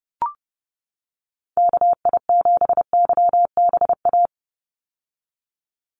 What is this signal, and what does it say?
Morse code, heard as a keyed tone, about 30 wpm, 700 Hz. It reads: KI7YBA
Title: USA Call Signs 30wpm